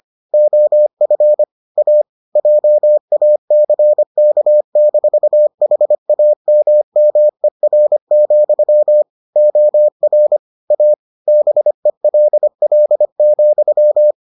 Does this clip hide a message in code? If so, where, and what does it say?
Yes: Morse code from 0.3 to 14.1 s, heard as a keyed tone, about 25 wpm, 600 Hz. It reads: OF A JACK-HAMMER, OR A BELL,